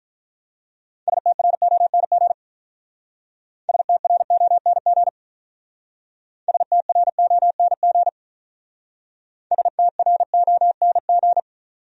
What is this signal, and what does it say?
Morse code, heard as a keyed tone, about 44 wpm, 700 Hz. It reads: STRONG STRONG STRONG STRONG